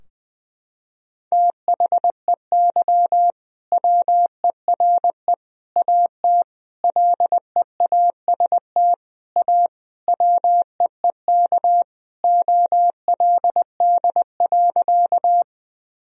Morse code, heard as a keyed tone, about 20 wpm, 700 Hz. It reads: THEY WERE AT LEAST A WEEK OLD.
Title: Sentences from Top 1000 Words - Review Part 2 - 20wpm